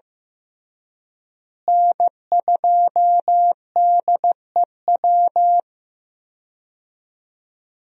Morse code, heard as a keyed tone, about 15 wpm, 700 Hz. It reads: N2DEW
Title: USA Call Signs - 5 Characters - Repeated 3x - 15wpm